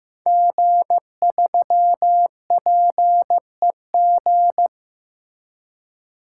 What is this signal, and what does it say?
Morse code, heard as a keyed tone, about 15 wpm, 700 Hz. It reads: G3PEG
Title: Code Groups - 5 Letter:Numbers - 15wpm